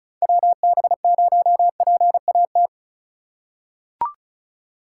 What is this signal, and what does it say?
Morse code, heard as a keyed tone, about 36 wpm, 690 Hz. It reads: WB0PAT
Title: USA Call Signs - 6 Characters - 2x Character Spacing - 35wpm